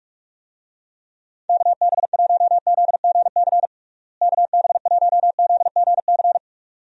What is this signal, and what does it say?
Morse code, heard as a keyed tone, about 42 wpm, 700 Hz. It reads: KB1ZGC KB1ZGC
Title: USA Call Signs - 6 Characters - Repeated 2x - 45wpm